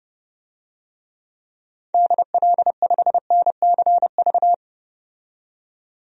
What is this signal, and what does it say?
Morse code, heard as a keyed tone, about 32 wpm, 700 Hz. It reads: DL5NCV
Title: DXCC-100 Call Signs - 6 Characters - 30wpm